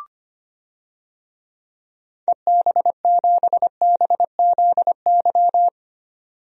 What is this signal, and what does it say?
Morse code, heard as a keyed tone, about 25 wpm, 700 Hz. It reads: EB7BZY
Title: DXCC-100 Call Signs - 6 Characters - 25wpm